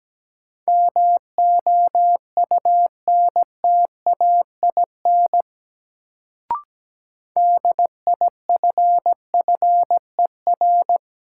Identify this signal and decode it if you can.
Morse code, heard as a keyed tone, about 17 wpm, 700 Hz. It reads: MOUNTAIN DIFFER
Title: Sentences from Top 1000 Words - Review Part 1 - New Word Drill - 17wpm